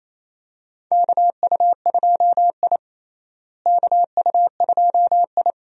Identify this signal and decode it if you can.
Morse code, heard as a keyed tone, about 28 wpm, 700 Hz. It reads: KU2I KU2I
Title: USA Call Signs - 4 Characters - Repeated 2x - 28wpm